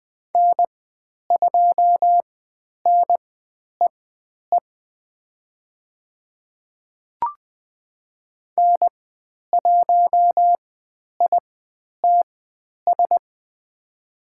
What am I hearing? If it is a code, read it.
Morse code, heard as a keyed tone, about 20 wpm, spaced as for 10 wpm, 700 Hz. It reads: N2NEE N1ITS
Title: 102 Lesson 17g - Letter u - Call Signs 20wpm